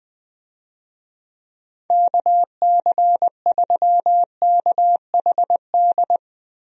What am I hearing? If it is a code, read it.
Morse code, heard as a keyed tone, about 21 wpm, 700 Hz. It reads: KC3KHD
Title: USA Call Signs - 6 Characters - Repeated 2x - 20wpm